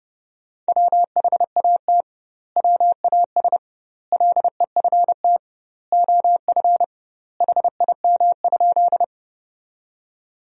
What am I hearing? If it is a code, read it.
Morse code, heard as a keyed tone, about 30 wpm, 700 Hz. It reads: WHAT WAS LEFT OF HIM?
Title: 2x Word-Spacing - Sentences from Top 300 Words - 30wpm